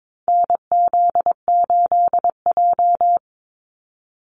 Morse code, heard as a keyed tone, about 22 wpm, 700 Hz. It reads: NZ8J
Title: Call Signs - 22wpm